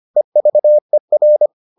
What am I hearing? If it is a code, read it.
Morse code, heard as a keyed tone, about 25 wpm, 600 Hz. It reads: EVER